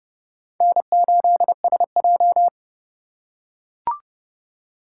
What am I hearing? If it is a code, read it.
Morse code, heard as a keyed tone, about 32 wpm, 700 Hz. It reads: N8SJ
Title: USA Call Signs - 3x Character Spacing - 30wpm